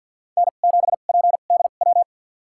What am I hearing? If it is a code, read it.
Morse code, heard as a keyed tone, about 53 wpm, spaced as for 37 wpm, 700 Hz. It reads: N7PDW
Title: USA Call Signs - 2x Character Spacing - 50wpm